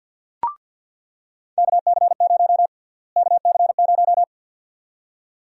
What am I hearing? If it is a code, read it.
Morse code, heard as a keyed tone, about 48 wpm, 700 Hz. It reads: KC0 KC0
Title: USA Call Sign Prefixes - Repeated 2x - 50wpm